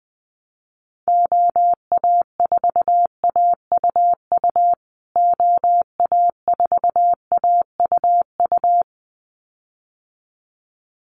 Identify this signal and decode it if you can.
Morse code, heard as a keyed tone, about 20 wpm, 700 Hz. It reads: OA4AUU OA4AUU
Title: USA Call Signs Repeated 2x 20wpm